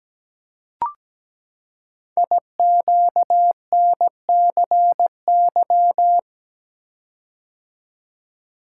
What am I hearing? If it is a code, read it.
Morse code, heard as a keyed tone, about 17 wpm, 700 Hz. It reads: IQNCY
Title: Code Groups - 5 Letters - 17wpm